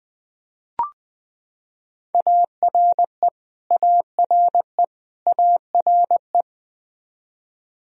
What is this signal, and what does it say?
Morse code, heard as a keyed tone, about 20 wpm, 700 Hz. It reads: ARE ARE ARE
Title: Top 100 Words Repeated 20wpm